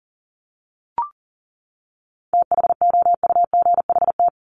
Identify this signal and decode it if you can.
Morse code, heard as a keyed tone, about 40 wpm, 700 Hz. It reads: THOUGHT
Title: Top 500 Words 40wpm